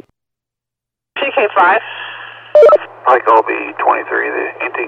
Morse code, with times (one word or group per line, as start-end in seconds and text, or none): none